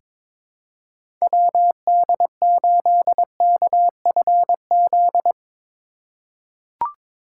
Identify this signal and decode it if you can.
Morse code, heard as a keyed tone, about 22 wpm, 700 Hz. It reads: WD8KFZ
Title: USA Call Signs - 6 Characters - 2x Character Spacing - 22wpm